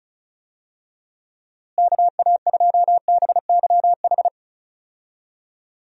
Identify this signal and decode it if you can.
Morse code, heard as a keyed tone, about 34 wpm, 700 Hz. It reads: KA2BYH
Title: USA Call Signs - 6 Characters - Repeated 2x - 35wpm